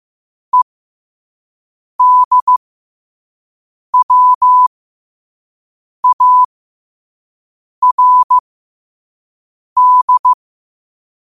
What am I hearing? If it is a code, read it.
Morse code, heard as a keyed tone, about 15 wpm, spaced as for 5 wpm, 1000 Hz. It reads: EDWARD
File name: qotdfiles / 5wpm